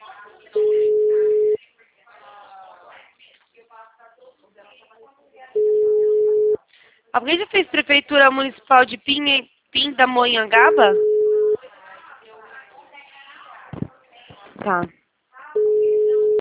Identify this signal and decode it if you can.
Ringback tone (Europe, 425 Hz), with speech.